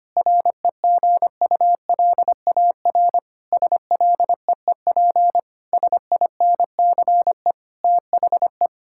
Morse code, heard as a keyed tone, about 25 wpm, 700 Hz. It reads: REGULAR SLEEP SINCE THE
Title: Sentences from Top 1000 Words - Review Part 6 - 25wpm